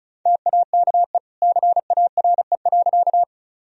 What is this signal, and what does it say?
Morse code, heard as a keyed tone, about 36 wpm, 700 Hz. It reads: TAKE CARE.
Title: Sentences from Top 1000 Words - Review Part 1 - 35wpm